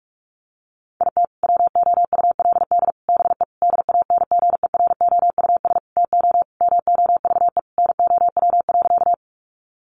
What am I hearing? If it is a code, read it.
Morse code, heard as a keyed tone, about 44 wpm, 700 Hz. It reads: IT WOULD BE DANGEROUS TO MOVE NOW.
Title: Sentences from Top 1200 Words 45wpm